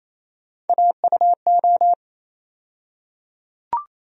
Morse code, heard as a keyed tone, about 28 wpm, 700 Hz. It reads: AUO
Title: USA Call Sign Postfixes - 3 Letters 28wpm